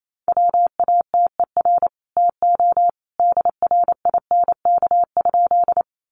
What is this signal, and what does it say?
Morse code, heard as a keyed tone, about 28 wpm, 700 Hz. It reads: WATER TO DRINK?
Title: Sentences from Top 1500 Words 28wpm